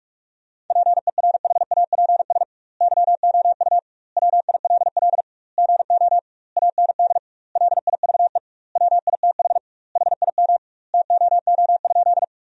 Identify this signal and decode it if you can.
Morse code, heard as a keyed tone, about 43 wpm, 700 Hz. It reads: PERHAPS YOU WILL GO AND LIVE WITH HIM TOO?
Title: 2x Word-Spacing - Sentences from Top 500 Words - 45wpm